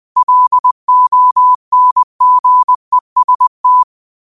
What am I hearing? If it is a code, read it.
Morse code, heard as a keyed tone, about 20 wpm, 1000 Hz. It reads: LONGEST